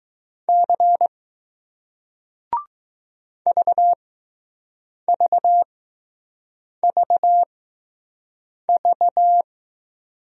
Morse code, heard as a keyed tone, about 20 wpm, spaced as for 6 wpm, 700 Hz. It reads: C VVVV